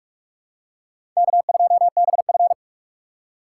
Morse code, heard as a keyed tone, about 46 wpm, 700 Hz. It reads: K2BF
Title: USA Call Signs - 4 Characters - 2x Character Spacing - 45wpm